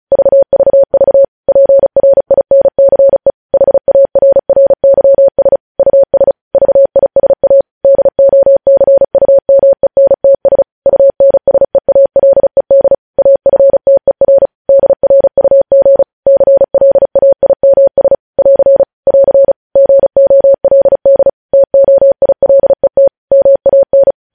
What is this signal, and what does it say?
Morse code, heard as a keyed tone, about 36 wpm, 550 Hz. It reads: VVV PRINCE HARRYS US VISA DOCUMENTS UNSEALED AFTER DRUG CLAIMS <AR> <AR> GOLD TOILET MAN
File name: podcast / News headlines-1x-35x35